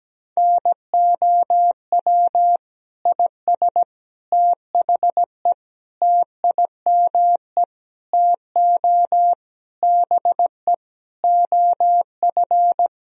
Morse code, heard as a keyed tone, about 17 wpm, 700 Hz. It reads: NOW IS THE TIME TO BE OF
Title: Sentences from Top 1000 Words - Review Part 4 - 17wpm